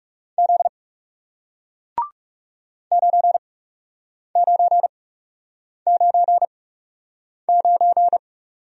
Morse code, heard as a keyed tone, about 40 wpm, spaced as for 8 wpm, 700 Hz. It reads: Z 9999